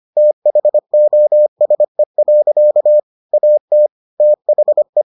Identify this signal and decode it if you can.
Morse code, heard as a keyed tone, about 25 wpm, 600 Hz. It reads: THOSE. AT THE